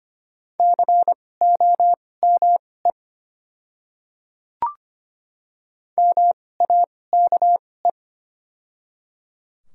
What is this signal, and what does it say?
Morse code, heard as a keyed tone, about 25 wpm, spaced as for 18 wpm, 690 Hz. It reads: COME MAKE
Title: 2x Character Spacing - Top 200 Words - 25wpm